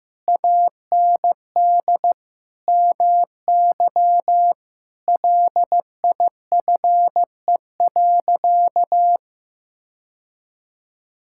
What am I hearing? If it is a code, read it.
Morse code, heard as a keyed tone, about 16 wpm, 700 Hz. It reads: AND MY LIFE.
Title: Sentences from Top 1000 Words - Review Part 5 - 15wpm